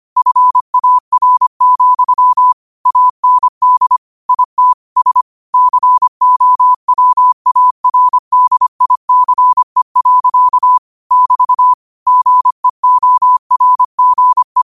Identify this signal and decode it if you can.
Morse code, heard as a keyed tone, about 25 wpm, 1000 Hz. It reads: FAR, AND IT S COWARDICE. <BT> GEORGE
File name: qotdfiles / 25wpm